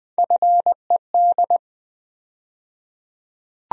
Morse code, heard as a keyed tone, about 20 wpm, 700 Hz. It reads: FED
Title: USA Call Sign Postfixes - 3 Letters 20wpm